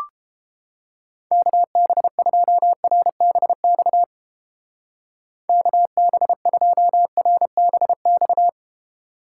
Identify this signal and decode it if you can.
Morse code, heard as a keyed tone, about 31 wpm, 700 Hz. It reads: KB2RBX KB2RBX